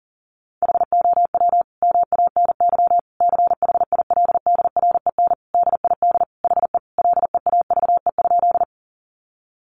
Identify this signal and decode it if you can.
Morse code, heard as a keyed tone, about 40 wpm, 700 Hz. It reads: HOW MANY CHILDREN DID HE LEAVE?
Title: Sentences from Top 400 Words 40wpm